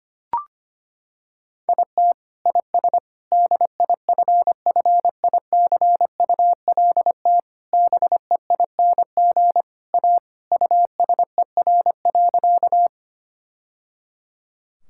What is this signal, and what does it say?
Morse code, heard as a keyed tone, about 25 wpm, 700 Hz. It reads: IT IS DIFFICULT BEING A USER.
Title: Sentences from Top 1000 Words - Review Part 5 - 25wpm